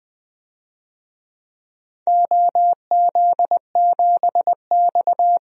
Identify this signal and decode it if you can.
Morse code, heard as a keyed tone, about 20 wpm, 700 Hz. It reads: OZ7X
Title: DXCC-100 Call Signs - 4 Characters - Repeated 3x - 20wpm